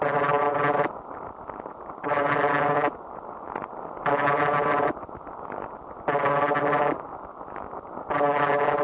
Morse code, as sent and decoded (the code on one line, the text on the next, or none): none